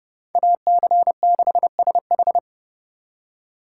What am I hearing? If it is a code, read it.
Morse code, heard as a keyed tone, about 37 wpm, 700 Hz. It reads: AC6SH
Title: Speed Racing - USA 2x2 Call Signs 30wpm